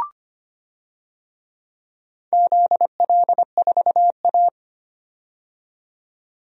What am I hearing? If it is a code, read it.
Morse code, heard as a keyed tone, about 25 wpm, 700 Hz. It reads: ZL4A
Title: DXCC-100 Call Signs - 4 Characters - 25wpm